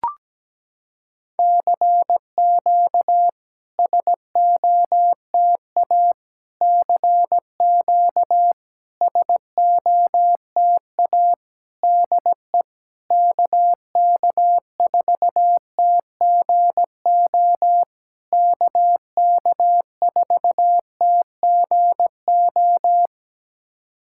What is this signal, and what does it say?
Morse code, heard as a keyed tone, about 17 wpm, 700 Hz. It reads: CQ SOTA CQ SOTA DE KK4TGO KK4TGO
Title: SOTA QSO Practice 17wpm